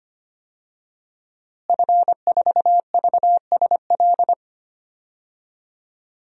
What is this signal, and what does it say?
Morse code, heard as a keyed tone, about 25 wpm, 700 Hz. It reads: F4VSL
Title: DXCC-100 Call Signs - 5 Characters - Repeated 2x - 25wpm